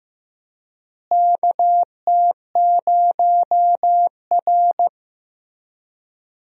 Morse code, heard as a keyed tone, about 15 wpm, 700 Hz. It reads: KT0R
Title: USA Call Signs - 4 Characters - Repeated 3x - 15wpm